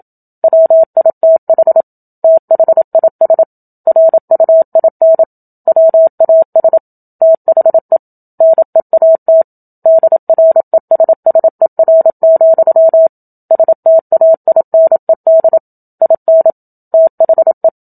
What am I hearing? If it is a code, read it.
Morse code, heard as a keyed tone, about 27 wpm, 650 Hz. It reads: WITH THIS RUIN WAS THE NEAT DRESSER, STAINED IN THE
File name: Book Two / 01-Chapter 1 - 27 WPM